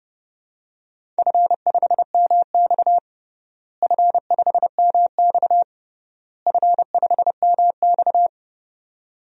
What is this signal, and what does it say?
Morse code, heard as a keyed tone, about 32 wpm, 700 Hz. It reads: F5MX F5MX F5MX
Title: DXCC-100 Call Signs - 4 Characters - Repeated 3x - 30wpm